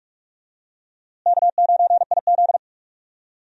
Morse code, heard as a keyed tone, about 46 wpm, 700 Hz. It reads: K9IZ
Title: USA Call Signs - 4 Characters - Repeated 2x - 45wpm